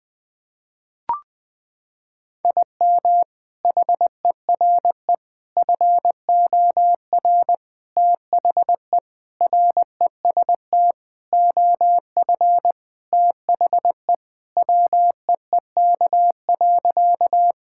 Morse code, heard as a keyed tone, about 20 wpm, 700 Hz. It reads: IM HERE FOR THE REST OF THE WEEK.
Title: Sentences with 4-Word Phrases - 20wpm